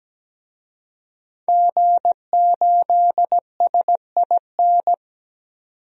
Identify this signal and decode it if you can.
Morse code, heard as a keyed tone, about 17 wpm, 700 Hz. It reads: G8SIN
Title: DXCC-100 Call Signs - 5 Characters - 17wpm